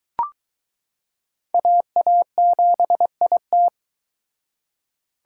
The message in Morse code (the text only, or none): AA7IT